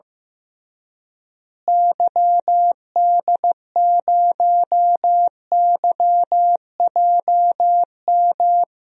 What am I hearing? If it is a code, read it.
Morse code, heard as a keyed tone, about 15 wpm, 700 Hz. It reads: YD0YJM
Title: DXCC-100 Call Signs - 6 Characters - Repeated 3x - 15wpm